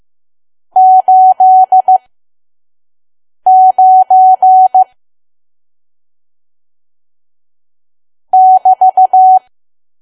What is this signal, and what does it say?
Morse code, heard as a keyed tone, about 15 wpm, spaced as for 5 wpm, 750 Hz. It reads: 89 <BT>
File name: Archive / 5 WPM files / PM